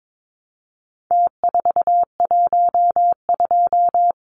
Morse code, heard as a keyed tone, about 22 wpm, 700 Hz. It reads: T412
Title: Numbers - Contest Style - 4-Digit 22wpm